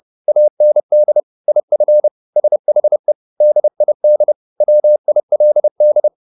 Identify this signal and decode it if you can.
Morse code, heard as a keyed tone, about 30 wpm, 600 Hz. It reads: AND IF SHE DID WILD
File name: the old man and the sea / the old man and the sea 1 30wpm